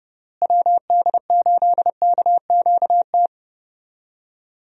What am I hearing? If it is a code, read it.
Morse code, heard as a keyed tone, about 30 wpm, 700 Hz. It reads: WD8KQT
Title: USA Call Signs - 6 Characters - Repeated 3x - 30wpm